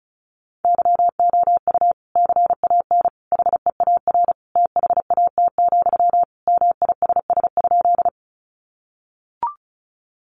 Morse code, heard as a keyed tone, about 34 wpm, 700 Hz. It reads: YOU CAN HEAR THAT, MISS?
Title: Sentences from Top 500 Words 35wpm